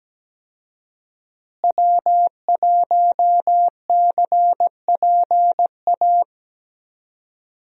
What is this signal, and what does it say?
Morse code, heard as a keyed tone, about 17 wpm, 700 Hz. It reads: W1CPA
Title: USA Call Signs - 5 Characters - Repeated 2x - 17wpm